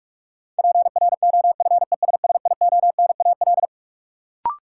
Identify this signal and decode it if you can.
Morse code, heard as a keyed tone, about 46 wpm, 700 Hz. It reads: PROFESSIONAL
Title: Sentences from Top 1000 Words - Review Part 4 - New Word Drill - 45wpm